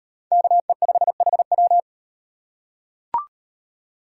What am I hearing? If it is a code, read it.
Morse code, heard as a keyed tone, about 36 wpm, 700 Hz. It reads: KE5HW